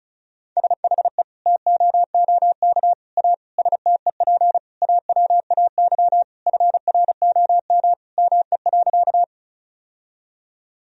Morse code, heard as a keyed tone, about 34 wpm, 700 Hz. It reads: SHE TOOK A STEP AWAY FROM ME.
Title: Sentences from Top 1000 Words - Review Part 2 - 35wpm